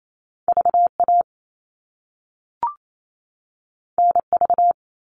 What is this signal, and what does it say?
Morse code, heard as a keyed tone, about 28 wpm, 700 Hz. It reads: VA NV
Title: All Cut Numbers - 2-Digit 28wpm